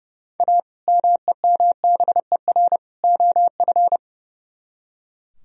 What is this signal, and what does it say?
Morse code, heard as a keyed tone, about 39 wpm, 700 Hz. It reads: A MEMBER OF